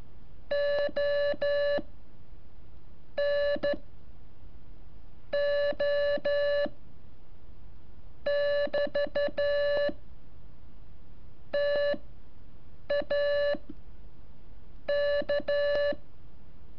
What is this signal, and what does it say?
Morse code, heard as a keyed tone, about 13 wpm, spaced as for 7 wpm, 580 Hz. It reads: O N O <BT> TA K
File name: curso / ejercicio 17